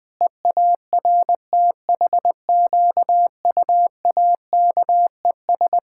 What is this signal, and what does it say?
Morse code, heard as a keyed tone, about 20 wpm, 700 Hz. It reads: EARTHQUAKES
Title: Pangram Challenge v2 - 20wpm